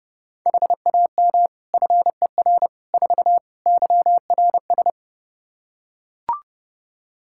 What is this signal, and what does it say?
Morse code, heard as a keyed tone, about 28 wpm, 700 Hz. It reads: HAM FER 4 YRS
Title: Practice QSO - Exchange 3 - More Info - 30wpm